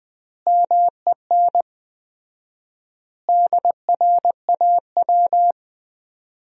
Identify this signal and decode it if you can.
Morse code, heard as a keyed tone, about 20 wpm, 700 Hz. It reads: MEN DRAW
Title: 4x Word-Spacing - Sentences from Top 300 Words - 20wpm